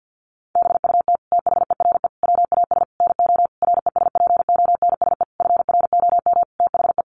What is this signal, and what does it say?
Morse code, heard as a keyed tone, about 50 wpm, 700 Hz. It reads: BUT THERE WAS NO RESPONSE FROM THE
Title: Sentences from Top 1400 Words 50wpm